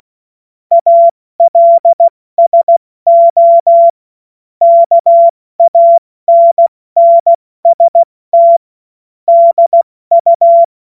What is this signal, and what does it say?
Morse code, heard as a keyed tone, about 16 wpm, 680 Hz. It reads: ALSO KANNST DU
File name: Morsekurs / Witze / Witz 16. Februar 2021, 70 F80